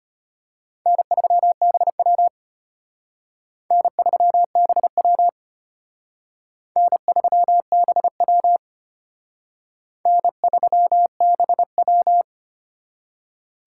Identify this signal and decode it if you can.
Morse code, heard as a keyed tone, about 31 wpm, 700 Hz. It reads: N3BW N3BW N3BW N3BW
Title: Speed Racing - USA 1x2 Call Signs 25wpm